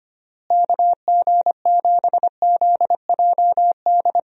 Morse code, heard as a keyed tone, about 25 wpm, 700 Hz. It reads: KG7ZJD
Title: USA Call Signs - 6 Characters - Repeated 3x - 25wpm